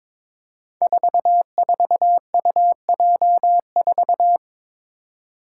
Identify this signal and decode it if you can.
Morse code, heard as a keyed tone, about 22 wpm, 700 Hz. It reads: 44UJ4
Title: Five Challenging Letter-Number Combinations 22wpm